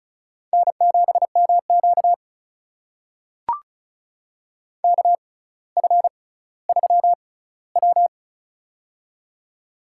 Morse code, heard as a keyed tone, about 34 wpm, spaced as for 18 wpm, 700 Hz. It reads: N7MQ KF3W
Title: USA Call Signs - 4 Characters - 6x Character Spacing - 35wpm